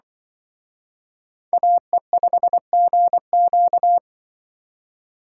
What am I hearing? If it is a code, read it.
Morse code, heard as a keyed tone, about 24 wpm, 700 Hz. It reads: AE5GQ
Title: Speed Racing - USA 2x2 Call Signs 20wpm